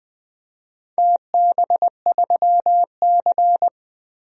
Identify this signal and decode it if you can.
Morse code, heard as a keyed tone, about 20 wpm, 700 Hz. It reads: TB3C
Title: DXCC-100 Call Signs - 4 Characters - Repeated 2x - 20wpm